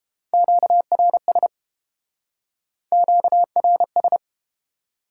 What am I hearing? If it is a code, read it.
Morse code, heard as a keyed tone, about 28 wpm, 700 Hz. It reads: QRS QRS